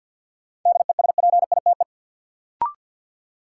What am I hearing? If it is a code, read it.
Morse code, heard as a keyed tone, about 53 wpm, 700 Hz. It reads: DESPITE
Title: Sentences from Top 1000 Words - Review Part 7 - New Word Drill - 50wpm